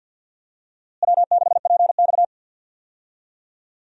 Morse code, heard as a keyed tone, about 52 wpm, 700 Hz. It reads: W6PX
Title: USA Call Signs - 4 Characters - Repeated 3x - 50wpm